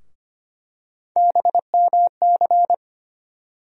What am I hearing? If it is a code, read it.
Morse code, heard as a keyed tone, about 25 wpm, 700 Hz. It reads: BMC
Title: USA Call Sign Postfixes - 3 Letters 25wpm